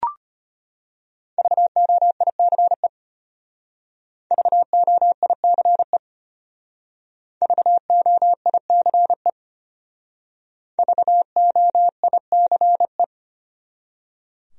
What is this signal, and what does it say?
Morse code, heard as a keyed tone, about 32 wpm, 700 Hz. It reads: VOICE VOICE VOICE VOICE